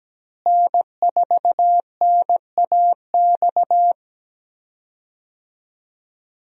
Morse code, heard as a keyed tone, about 17 wpm, 700 Hz. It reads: N4NAX
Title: USA Call Signs - 5 Characters - Repeated 3x - 17wpm